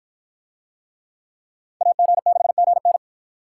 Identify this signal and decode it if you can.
Morse code, heard as a keyed tone, about 53 wpm, 700 Hz. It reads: AG6GN